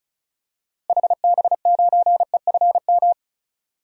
Morse code, heard as a keyed tone, about 36 wpm, 700 Hz. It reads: HB9EFM
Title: DXCC-100 Call Signs - 6 Characters - Repeated 2x - 35wpm